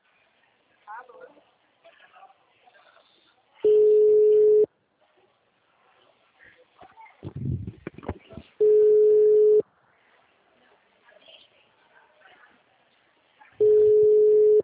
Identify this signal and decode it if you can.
Ringback tone (Europe, 425 Hz), with speech.
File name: out- -2010- - - .78536